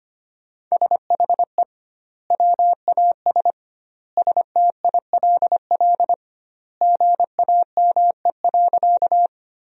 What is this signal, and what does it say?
Morse code, heard as a keyed tone, about 25 wpm, 700 Hz. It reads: SHE WAS STILL GAME.
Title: 2x Word-Spacing - Sentences from Top 500 Words - 25wpm